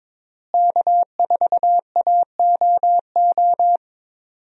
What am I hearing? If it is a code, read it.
Morse code, heard as a keyed tone, about 22 wpm, 700 Hz. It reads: K4AOO
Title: USA Call Signs - 5 Characters - Repeated 3x - 22wpm